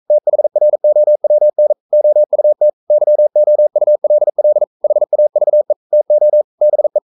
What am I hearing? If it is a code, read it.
Morse code, heard as a keyed tone, about 42 wpm, 600 Hz. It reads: THROWN OUT YOULL HAVE TO BE
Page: Romanen Never af Ken Follet på engelsk.